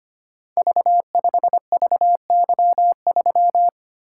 Morse code, heard as a keyed tone, about 25 wpm, 700 Hz. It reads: V5VY3
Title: Code Groups - 5 Letter:Numbers - 25wpm